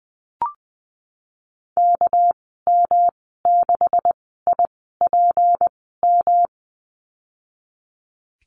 Morse code, heard as a keyed tone, about 20 wpm, spaced as for 15 wpm, 700 Hz. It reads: KM6IPM
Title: 238 Lesson 38g - Letter q - Call Signs 20wpm